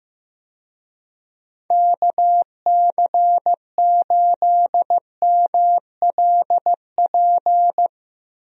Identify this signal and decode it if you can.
Morse code, heard as a keyed tone, about 15 wpm, 700 Hz. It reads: KC8MLP
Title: USA Call Signs - 6 Characters - Repeated 3x - 15wpm